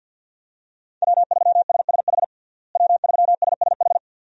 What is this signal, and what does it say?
Morse code, heard as a keyed tone, about 52 wpm, 700 Hz. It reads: W3SSH W3SSH
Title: USA Call Signs - 5 Characters - Repeated 2x - 50wpm